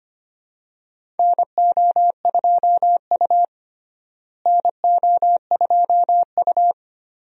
Morse code, heard as a keyed tone, about 25 wpm, 700 Hz. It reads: NO2U NO2U
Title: USA Call Signs - 4 Characters - Repeated 3x - 25wpm